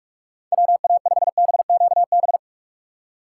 Morse code, heard as a keyed tone, about 46 wpm, 700 Hz. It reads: WA5BQB
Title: USA Call Signs - 6 Characters - 2x Character Spacing - 45wpm